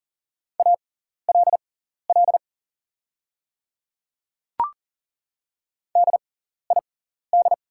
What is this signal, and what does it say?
Morse code, heard as a keyed tone, about 42 wpm, spaced as for 14 wpm, 690 Hz. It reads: ALL DID
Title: 6x Character Spacing - Top 100 Words - 40wpm